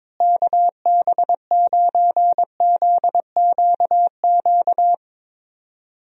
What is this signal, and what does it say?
Morse code, heard as a keyed tone, about 22 wpm, 700 Hz. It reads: KB9ZQQ
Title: USA Call Signs - 6 Characters - Repeated 2x - 22wpm